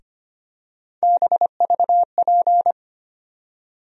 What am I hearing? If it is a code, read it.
Morse code, heard as a keyed tone, about 25 wpm, 700 Hz. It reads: BVP
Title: USA Call Sign Postfixes - 3 Letters 25wpm